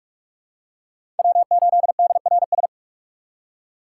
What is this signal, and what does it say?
Morse code, heard as a keyed tone, about 46 wpm, 700 Hz. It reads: W8DRS
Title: USA Call Signs - 5 Characters - 6x Character Spacing - 45wpm